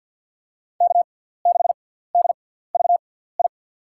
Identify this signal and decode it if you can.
Morse code, heard as a keyed tone, about 52 wpm, spaced as for 17 wpm, 700 Hz. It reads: K6DVI
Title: USA Call Signs - 5 Characters - 6x Character Spacing - 50wpm